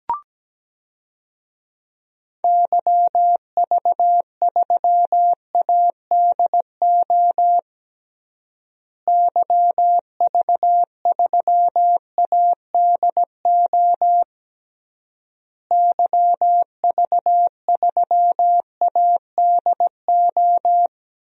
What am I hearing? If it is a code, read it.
Morse code, heard as a keyed tone, about 17 wpm, 700 Hz. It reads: YV3ADO YV3ADO YV3ADO
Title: DXCC-100 Call Signs - 6 Characters - Repeated 3x - 17wpm